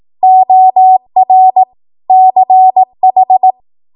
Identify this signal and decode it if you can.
Morse code, heard as a keyed tone, about 18 wpm, 750 Hz. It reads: OR CH
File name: Morse / 18WPM